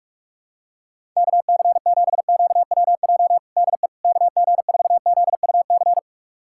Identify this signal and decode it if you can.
Morse code, heard as a keyed tone, about 42 wpm, 700 Hz. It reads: KC7QWJ DE KG4ZUC
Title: USA Callsign DE Callsign - 6 Characters - 45wpm